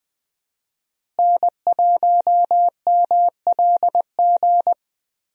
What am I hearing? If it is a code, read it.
Morse code, heard as a keyed tone, about 20 wpm, 700 Hz. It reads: N1MLG
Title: USA Call Signs - 5 Characters - Repeated 2x - 20wpm